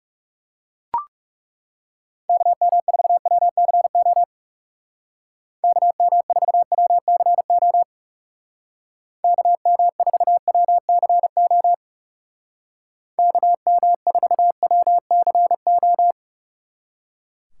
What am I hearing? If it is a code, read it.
Morse code, heard as a keyed tone, about 37 wpm, 700 Hz. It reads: KM4WCO KM4WCO KM4WCO KM4WCO
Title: Speed Racing - USA 2x3 Call Signs 30wpm